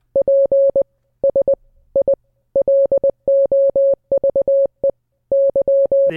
Morse code, heard as a keyed tone, about 20 wpm, 550 Hz. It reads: PS I LOVE Y